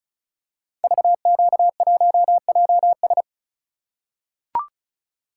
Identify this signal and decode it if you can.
Morse code, heard as a keyed tone, about 36 wpm, 700 Hz. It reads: VQ1JS
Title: Code Groups - 5 Letter:Numbers - 35wpm